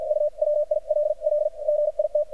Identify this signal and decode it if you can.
Morse code, heard as a keyed tone, about 57 wpm, spaced as for 46 wpm, 600 Hz. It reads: U WILL GIT
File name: archived / CRSnet